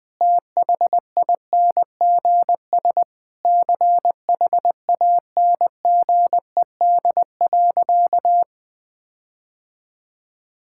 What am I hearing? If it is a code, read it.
Morse code, heard as a keyed tone, about 21 wpm, 700 Hz. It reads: THINGS CHANGED.
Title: Sentences with 4-Word Phrases - 20wpm